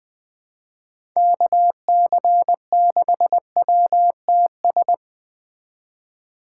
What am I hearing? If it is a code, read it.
Morse code, heard as a keyed tone, about 20 wpm, 700 Hz. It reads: KC6WTS
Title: Speed Racing - USA 2x3 Call Signs 17wpm